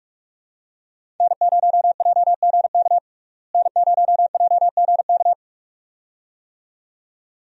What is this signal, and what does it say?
Morse code, heard as a keyed tone, about 46 wpm, 700 Hz. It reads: N0JGK N0JGK
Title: USA Call Signs - 5 Characters - Repeated 2x - 45wpm